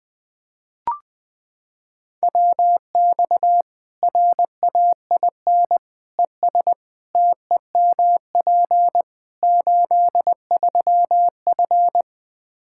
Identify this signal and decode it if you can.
Morse code, heard as a keyed tone, about 20 wpm, 700 Hz. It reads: WX RAIN ES TEMP 83F
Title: Practice QSO - Exchange 2 - Info - 20wpm